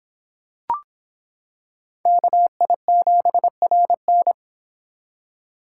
Morse code, heard as a keyed tone, about 26 wpm, 700 Hz. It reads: KI7RN